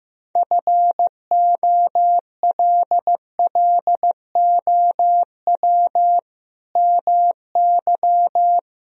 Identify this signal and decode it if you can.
Morse code, heard as a keyed tone, about 15 wpm, 700 Hz. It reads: FOLLOW MY
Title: Sentences from Top 1000 Words - Review Part 1 - 15wpm